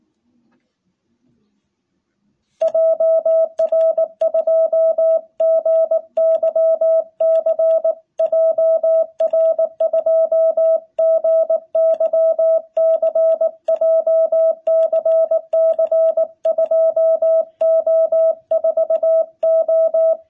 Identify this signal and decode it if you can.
Morse code, heard as a keyed tone, about 19 wpm, 660 Hz. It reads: JR2GYC JR2GYC JCC2O4O
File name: jh8cbh / 2021cw